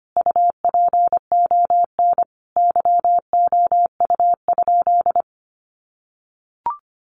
Morse code, heard as a keyed tone, about 25 wpm, 700 Hz. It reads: UPON YOU?
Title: Sentences from Top 1100 Words 25wpm